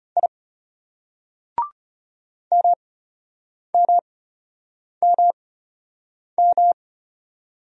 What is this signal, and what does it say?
Morse code, heard as a keyed tone, about 33 wpm, spaced as for 8 wpm, 700 Hz. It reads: I MMMM